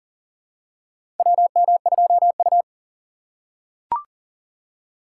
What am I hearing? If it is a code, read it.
Morse code, heard as a keyed tone, about 40 wpm, 700 Hz. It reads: WM2U
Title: USA Call Signs - 4 Characters - 6x Character Spacing - 40wpm